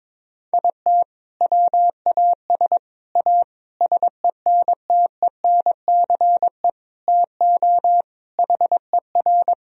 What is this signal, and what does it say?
Morse code, heard as a keyed tone, about 22 wpm, 700 Hz. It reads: IT WAS A SENTENCE TO HER
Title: Sentences from Top 1000 Words - Review Part 1 - 22wpm